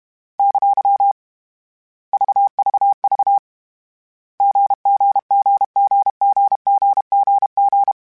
Morse code, heard as a keyed tone, about 32 wpm, 800 Hz. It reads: ! VVV GGGGGGGG